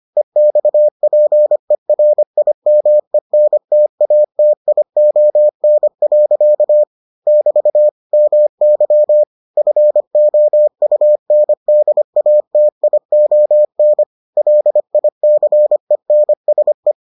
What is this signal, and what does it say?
Morse code, heard as a keyed tone, about 25 wpm, 600 Hz. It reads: EXPERIMENTATION. <BT> MY FOUNDATION LICENSE